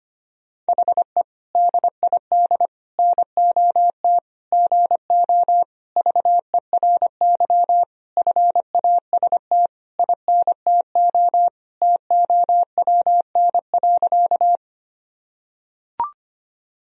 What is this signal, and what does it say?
Morse code, heard as a keyed tone, about 25 wpm, 700 Hz. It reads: HE DID NOT GO VERY FAST INTO TOWN.
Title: Sentences from Top 1000 Words - Review Part 2 - 25wpm